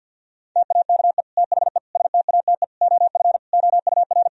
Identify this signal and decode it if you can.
Morse code, heard as a keyed tone, about 50 wpm, 700 Hz. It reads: TAKE THE STATE OF OUR
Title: Sentences from Top 1000 Words - Review Part 3 - 50wpm